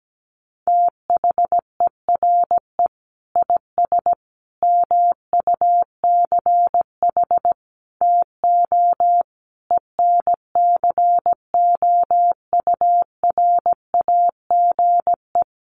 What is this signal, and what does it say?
Morse code, heard as a keyed tone, about 17 wpm, 700 Hz. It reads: THERE IS MUCH TO ENCOURAGE
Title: Sentences from Top 1400 Words 17wpm